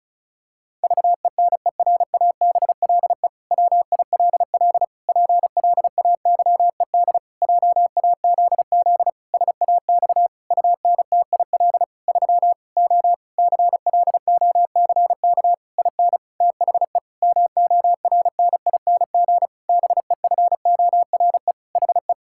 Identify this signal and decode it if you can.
Morse code, heard as a keyed tone, about 34 wpm, 700 Hz. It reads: VENERABLE WILL PLAYED JAZZ SAX UNTIL 3 O CLOCK IN THE MORNING BEFORE HE
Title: Pangram Challenge v2 - 35wpm